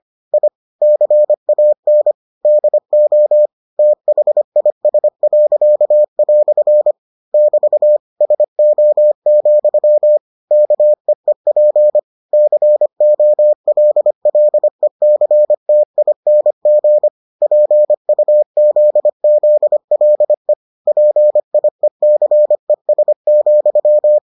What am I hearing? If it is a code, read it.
Morse code, heard as a keyed tone, about 25 wpm, 600 Hz. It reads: I CAN DO THIS." <BT> SO, KEEP COLLECTING PUZZLE PIECES,